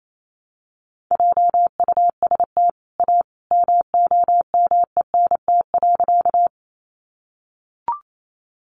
Morse code, heard as a keyed tone, about 28 wpm, 700 Hz. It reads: JUST A MOMENT.
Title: Sentences from Top 1000 Words 28wpm